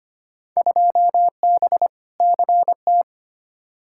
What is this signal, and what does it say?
Morse code, heard as a keyed tone, about 25 wpm, 700 Hz. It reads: 2B CT
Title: Field Day Exchanges Encore - 25wpm